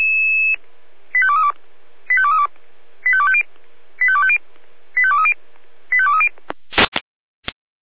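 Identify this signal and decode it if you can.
Five-tone selective call (ZVEI), heard as quick runs of five tones, tones of about 70 ms. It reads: 85232 85232 85250 85250 85230 85230
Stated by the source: Five-tone ZVEI selective call 85259, received on an SDR.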